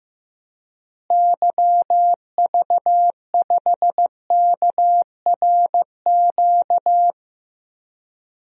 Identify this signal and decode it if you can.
Morse code, heard as a keyed tone, about 15 wpm, 700 Hz. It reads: YV5KRQ
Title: DXCC-100 Call Signs - 6 Characters - Repeated 2x - 15wpm